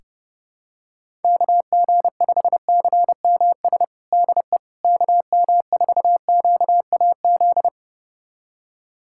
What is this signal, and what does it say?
Morse code, heard as a keyed tone, about 32 wpm, 700 Hz. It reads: KG5CMS DE KM4QAZ
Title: USA Callsign DE Callsign - 6 Characters - 30wpm